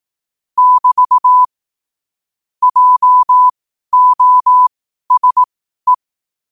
Morse code, heard as a keyed tone, about 18 wpm, spaced as for 12 wpm, 1000 Hz. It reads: <BT> JOSE
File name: qotdfiles / 12wpm